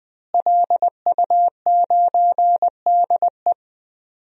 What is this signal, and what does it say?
Morse code, heard as a keyed tone, about 21 wpm, 700 Hz. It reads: LU9DE
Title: DXCC-100 Call Signs - 5 Characters - Repeated 3x - 20wpm